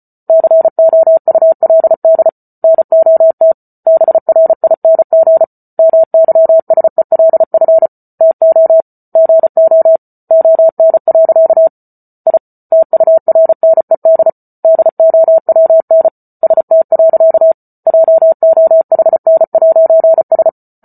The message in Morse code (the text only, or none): COULD NOT BRING MYSELF TO GO ON. I TURNED DOWN ST. JOHN'S